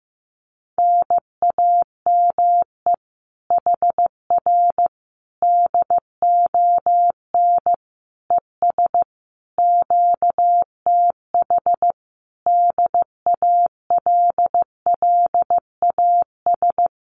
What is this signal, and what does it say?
Morse code, heard as a keyed tone, about 15 wpm, 700 Hz. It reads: NAME HR DON ES QTH DALLAS
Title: Common QSO Phrases Encore 15wpm